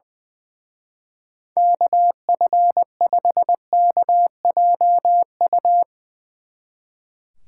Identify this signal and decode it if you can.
Morse code, heard as a keyed tone, about 24 wpm, 700 Hz. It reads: KF5KJU
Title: Speed Racing - USA 2x3 Call Signs 20wpm